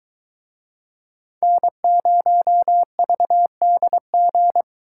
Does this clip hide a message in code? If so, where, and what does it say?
Yes: Morse code from 1.4 to 4.6 s, heard as a keyed tone, about 23 wpm, 700 Hz. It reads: N0VDG